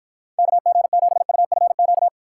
Morse code, heard as a keyed tone, about 52 wpm, 700 Hz. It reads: KG7UFQ